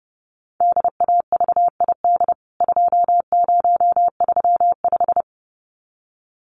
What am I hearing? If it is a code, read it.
Morse code, heard as a keyed tone, about 30 wpm, 700 Hz. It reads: DAVID 2035
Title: CWT Exchange 30wpm